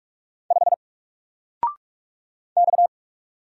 Morse code, heard as a keyed tone, about 44 wpm, 700 Hz. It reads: 5 X
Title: Single Character (Letter:Number:Punctuation:Prosign) 45wpm